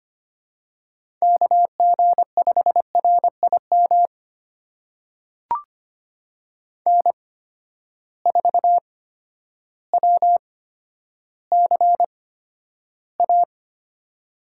Morse code, heard as a keyed tone, about 25 wpm, spaced as for 11 wpm, 700 Hz. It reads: KG5RIM N4WCA
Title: USA Call Signs - 8x Character Spacing - 25wpm